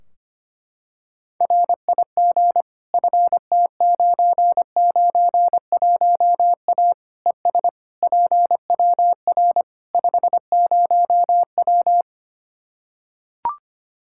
Morse code, heard as a keyed tone, about 25 wpm, 700 Hz. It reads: RIG FT991A ES PWR 50W
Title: Practice QSO - Exchange 2 - Info - 25wpm